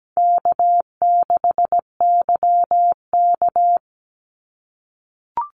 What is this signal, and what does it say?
Morse code, heard as a keyed tone, about 17 wpm, 700 Hz. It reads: K6YK
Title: NA SOTA Chaser Callsigns 17wpm